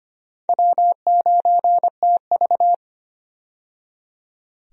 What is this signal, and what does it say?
Morse code, heard as a keyed tone, about 25 wpm, 700 Hz. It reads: W9TV
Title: USA Call Signs - 4 Characters - 25wpm